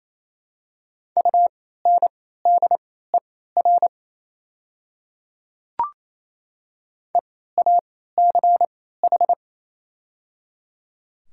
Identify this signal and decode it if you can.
Morse code, heard as a keyed tone, about 28 wpm, spaced as for 16 wpm, 700 Hz. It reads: UNDER EACH
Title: 3x Character Spacing - Top 200 Words - 28wpm